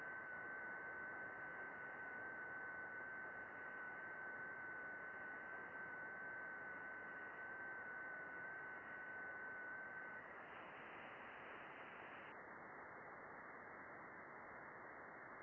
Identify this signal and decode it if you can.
FT8.
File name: contests / june-vhf-k5tr- -0200